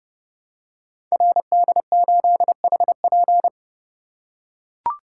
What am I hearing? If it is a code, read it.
Morse code, heard as a keyed tone, about 30 wpm, 700 Hz. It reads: RD8HP
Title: Code Groups - 5 Letter:Numbers - 30wpm